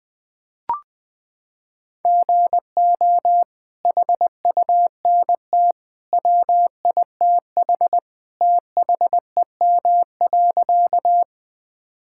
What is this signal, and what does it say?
Morse code, heard as a keyed tone, about 20 wpm, 700 Hz. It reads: GO HUNT WITH THEM.
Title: Sentences from Top 1600 Words 20wpm